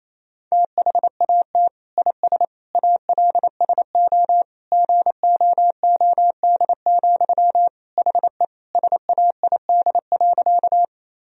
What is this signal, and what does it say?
Morse code, heard as a keyed tone, about 27 wpm, 700 Hz. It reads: THAT IS ALSO GOOD, HE SAID.
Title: Sentences from Top 1000 Words - Review Part 1 - 28wpm